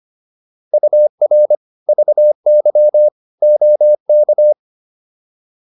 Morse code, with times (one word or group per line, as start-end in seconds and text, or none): 0.7-1.6 UR
1.9-3.1 VY
3.4-4.5 OK